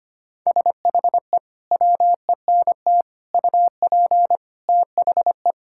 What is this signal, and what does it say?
Morse code, heard as a keyed tone, about 25 wpm, 700 Hz. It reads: SHE WENT UP THE
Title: Sentences from Top 1000 Words - Review Part 5 - 25wpm